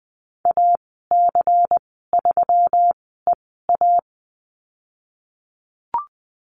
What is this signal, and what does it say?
Morse code, heard as a keyed tone, about 20 wpm, spaced as for 15 wpm, 700 Hz. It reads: AC3EA
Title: 122 Lesson 20g - Number 3 - Call Signs 20wpm